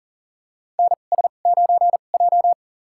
Morse code, heard as a keyed tone, about 40 wpm, spaced as for 29 wpm, 700 Hz. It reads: NS9J
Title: USA Call Signs - 4 Characters - 2x Character Spacing - 40wpm